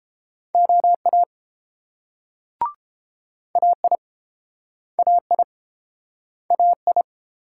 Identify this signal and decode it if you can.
Morse code, heard as a keyed tone, about 32 wpm, 700 Hz. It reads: OA AI AI AI